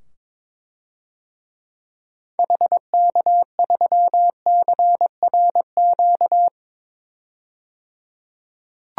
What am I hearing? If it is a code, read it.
Morse code, heard as a keyed tone, about 22 wpm, 700 Hz. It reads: HK3CRQ
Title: DXCC-100 Call Signs - 6 Characters - 22wpm